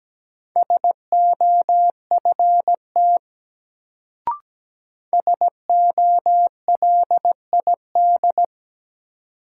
Morse code, heard as a keyed tone, about 17 wpm, 700 Hz. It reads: SOFT SOLID
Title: Sentences from Top 1500 Words - New Word Drill Alt - 17wpm